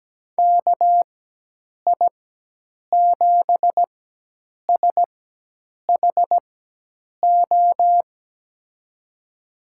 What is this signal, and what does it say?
Morse code, heard as a keyed tone, about 17 wpm, spaced as for 8 wpm, 700 Hz. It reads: KI7SHO
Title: USA Call Signs - 6 Characters - 4x Character Spacing - 17wpm